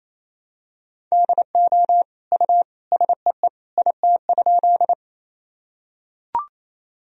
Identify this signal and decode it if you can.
Morse code, heard as a keyed tone, about 28 wpm, 700 Hz. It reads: DO U SEE IT?
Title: Sentences from Top 100 Words and Abbreviations 28wpm